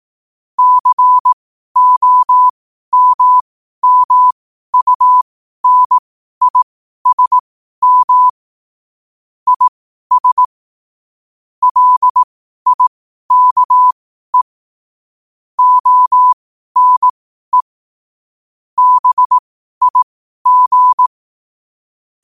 Morse code, heard as a keyed tone, about 18 wpm, spaced as for 12 wpm, 1000 Hz. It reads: COMMUNISM IS LIKE ONE BIG
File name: qotdfiles / 12wpm